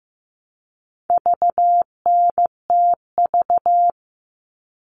Morse code, heard as a keyed tone, about 15 wpm, 700 Hz. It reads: VNTV
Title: All Cut Numbers - 4-Digit 15wpm